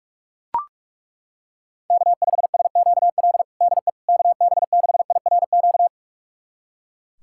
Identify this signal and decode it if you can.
Morse code, heard as a keyed tone, about 43 wpm, 700 Hz. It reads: K5SQL DE KB6IRQ
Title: USA Callsign DE Callsign 45wpm